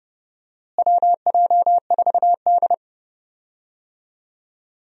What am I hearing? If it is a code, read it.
Morse code, heard as a keyed tone, about 30 wpm, 700 Hz. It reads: WJ4D
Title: USA Call Signs - 4 Characters - Repeated 3x - 30wpm